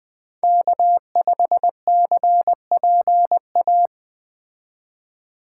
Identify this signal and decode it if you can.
Morse code, heard as a keyed tone, about 20 wpm, 700 Hz. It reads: K5CPA